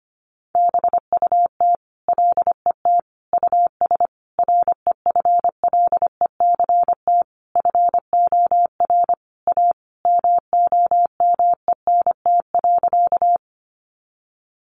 Morse code, heard as a keyed tone, about 25 wpm, 700 Hz. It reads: BUT LET US REFLECT FOR A MOMENT.
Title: Sentences from Top 1500 Words 25wpm